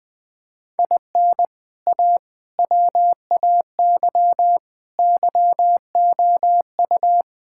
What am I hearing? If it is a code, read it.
Morse code, heard as a keyed tone, about 21 wpm, 700 Hz. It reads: IN A WAY YOU
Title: Sentences with 3-Word Phrases - 20wpm